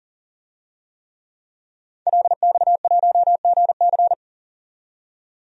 Morse code, heard as a keyed tone, about 40 wpm, 700 Hz. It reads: LX1GC
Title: DXCC-100 Call Signs - 5 Characters - 40wpm